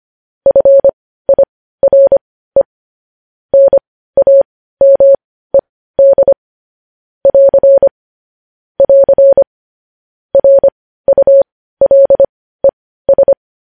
Morse code, heard as a keyed tone, about 25 wpm, spaced as for 15 wpm, 550 Hz. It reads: FIRE NAMED <AR> <AR> RULES
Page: news-headlines-in-morse-code-at-15-wpm